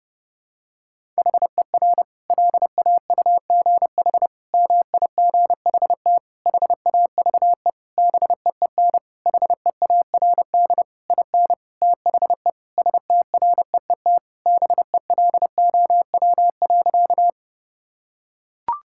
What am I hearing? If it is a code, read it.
Morse code, heard as a keyed tone, about 30 wpm, 700 Hz. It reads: HER LAUGH MIGHT HAVE BEEN HEARD IN THE STREET BELOW.
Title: Sentences from Top 1000 Words - Review Part 5 - 30wpm